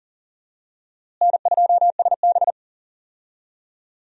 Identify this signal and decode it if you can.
Morse code, heard as a keyed tone, about 39 wpm, 700 Hz. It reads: N2SB
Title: USA Call Signs - 4 Characters - Repeated 2x - 40wpm